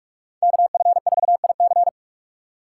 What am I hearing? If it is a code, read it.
Morse code, heard as a keyed tone, about 46 wpm, 700 Hz. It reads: KF4IC
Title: USA Call Signs - 5 Characters - 4x Character Spacing - 45wpm